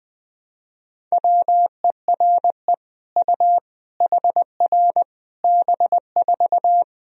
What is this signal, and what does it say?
Morse code, heard as a keyed tone, about 20 wpm, 700 Hz. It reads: WERE U HR B4
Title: Sentences from Top 200 Words and Abbreviations 20wpm